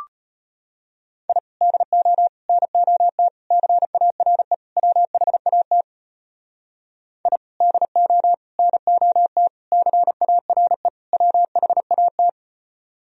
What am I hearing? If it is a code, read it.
Morse code, heard as a keyed tone, about 32 wpm, 700 Hz. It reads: I DO NOT CARE WHAT I DO NOT CARE WHAT